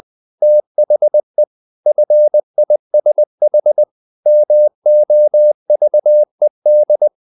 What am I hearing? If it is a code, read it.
Morse code, heard as a keyed tone, about 20 wpm, 600 Hz. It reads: THE FISH MOVED
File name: the old man and the sea / the old man and the sea part 1 20wpm